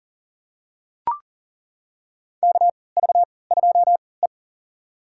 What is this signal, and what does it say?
Morse code, heard as a keyed tone, about 40 wpm, spaced as for 23 wpm, 700 Hz. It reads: KV2E
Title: USA Call Signs - 4 Characters - 3x Character Spacing - 40wpm